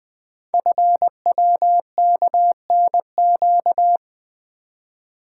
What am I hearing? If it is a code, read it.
Morse code, heard as a keyed tone, about 20 wpm, 700 Hz. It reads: FWKNQ
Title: Code Groups - 5 Letters - 20wpm